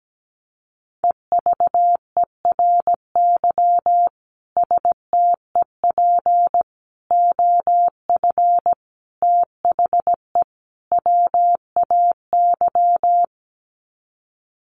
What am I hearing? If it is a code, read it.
Morse code, heard as a keyed tone, about 17 wpm, 700 Hz. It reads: EVERY STEP OF THE WAY
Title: Sets of 5 Words Encore2 17wpm